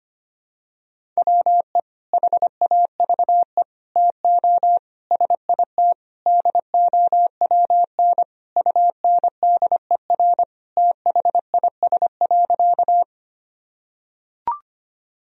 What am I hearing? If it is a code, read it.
Morse code, heard as a keyed tone, about 25 wpm, 700 Hz. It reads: WE HAVE TO SIT DOWN UNDER THIS.
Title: Sentences from Top 1000 Words - Review Part 1 - 25wpm